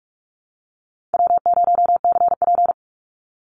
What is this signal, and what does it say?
Morse code, heard as a keyed tone, about 44 wpm, 700 Hz. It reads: W0CP
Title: Easy Call Signs - 45wpm